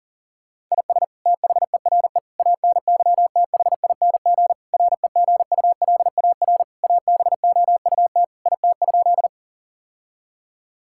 Morse code, heard as a keyed tone, about 40 wpm, 700 Hz. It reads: IS THERE ANYTHING REGULAR ABOUT IT?
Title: Sentences from Top 1000 Words - Review Part 6 - 40wpm